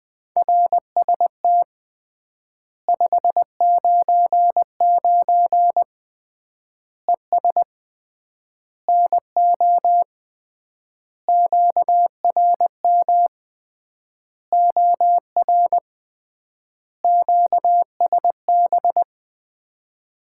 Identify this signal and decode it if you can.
Morse code, heard as a keyed tone, about 20 wpm, 700 Hz. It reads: RST 599 ES NO QRM OR QSB
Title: Common QSO Phrases - 3x Word-Spacing - 20wpm